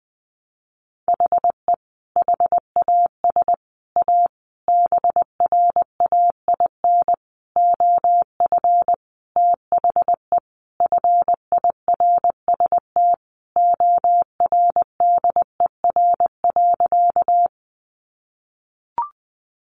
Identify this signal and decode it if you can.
Morse code, heard as a keyed tone, about 20 wpm, 700 Hz. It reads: HE HAS A BRAIN OF THE FIRST ORDER.
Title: Sentences from Top 1100 Words 20wpm